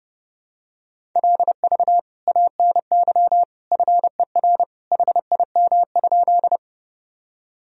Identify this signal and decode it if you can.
Morse code, heard as a keyed tone, about 30 wpm, 700 Hz. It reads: LV ANY FER HIM?
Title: Sentences from Top 400 Words and Abbreviations 30wpm